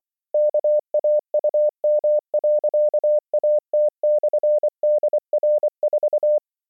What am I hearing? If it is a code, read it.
Morse code, heard as a keyed tone, about 24 wpm, 600 Hz. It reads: KAUM.AT/DR4